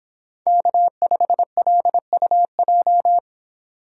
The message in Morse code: K5LUJ

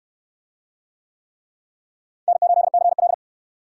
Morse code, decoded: N8CL

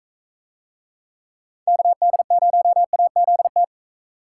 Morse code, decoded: KD0AZT

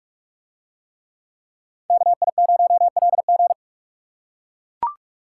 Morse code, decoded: KI0LG

